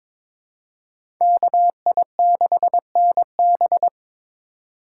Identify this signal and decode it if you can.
Morse code, heard as a keyed tone, about 22 wpm, 700 Hz. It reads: KI6NB
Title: USA Call Signs - 5 Characters - 2x Character Spacing - 22wpm